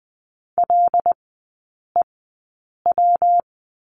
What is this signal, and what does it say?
Morse code, heard as a keyed tone, about 20 wpm, spaced as for 8 wpm, 700 Hz. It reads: LEW
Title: 133 Lesson 22e - Question Mark - Three Characters 20wpm